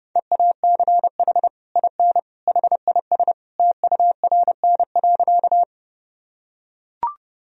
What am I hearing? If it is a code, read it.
Morse code, heard as a keyed tone, about 31 wpm, 700 Hz. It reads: EACH IN HIS TURN.
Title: Sentences from Top 1000 Words - Review Part 1 - 30wpm